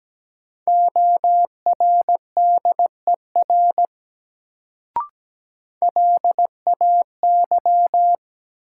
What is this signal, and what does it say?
Morse code, heard as a keyed tone, about 17 wpm, 700 Hz. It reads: ORDER LAY
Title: Sentences from Top 400 Words - New Word Drill - 17wpm